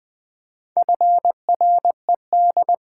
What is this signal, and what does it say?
Morse code, heard as a keyed tone, about 21 wpm, 700 Hz. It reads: FRED
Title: Pangram Challenge v2 - 20wpm